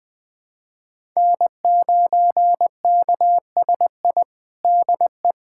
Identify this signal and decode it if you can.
Morse code, heard as a keyed tone, about 20 wpm, 700 Hz. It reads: N9KSI DE
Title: USA Callsign DE Callsign - 5 Characters - 20wpm